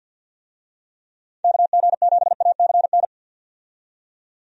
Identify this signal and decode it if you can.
Morse code, heard as a keyed tone, about 48 wpm, 700 Hz. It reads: KG7ACN
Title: USA Call Signs - 6 Characters - 50wpm